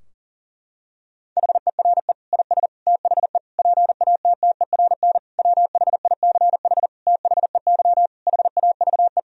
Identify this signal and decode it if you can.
Morse code, heard as a keyed tone, about 40 wpm, 700 Hz. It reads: HERE IS THE PATTERN WHICH THEY HAVE
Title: Sentences from Top 1000 Words - Review Part 2 - 40wpm